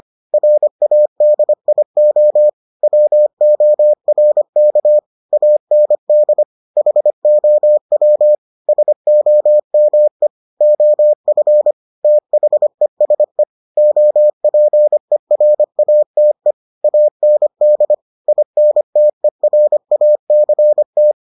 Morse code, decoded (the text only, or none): RADIO WORK AND HOW SOME OF THESE OPERATE AND INTERACT